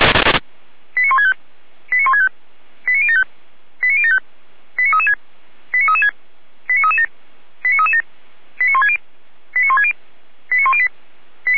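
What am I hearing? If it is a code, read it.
Five-tone selective call (ZVEI), heard as quick runs of five tones, tones of about 70 ms. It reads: 89165 89165 89075 89075 89336 89336 89337 89337 89160 89160 89108
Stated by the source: ZVEI five-tone call 89123 received on SDR, recorded by fire brigade Feuerwehr Orlamünde.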